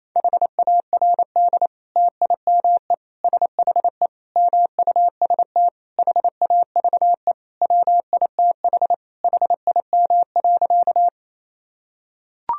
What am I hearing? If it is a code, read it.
Morse code, heard as a keyed tone, about 28 wpm, 700 Hz. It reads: HARD TIME SHE MUST HAVE WITH HIM.
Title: Sentences from Top 1000 Words - Review Part 1 - 28wpm